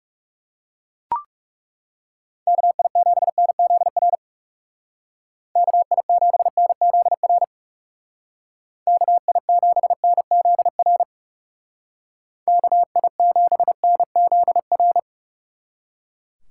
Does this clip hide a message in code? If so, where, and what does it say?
Yes: Morse code from 2.5 to 15.0 s, heard as a keyed tone, about 40 wpm, 700 Hz. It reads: KI7NZR KI7NZR KI7NZR KI7NZR